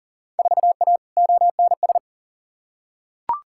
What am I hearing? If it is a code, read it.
Morse code, heard as a keyed tone, about 40 wpm, 700 Hz. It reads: 4A ONS
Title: Field Day Exchanges - 40wpm